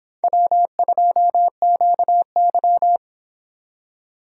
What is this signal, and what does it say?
Morse code, heard as a keyed tone, about 26 wpm, 700 Hz. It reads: W2QY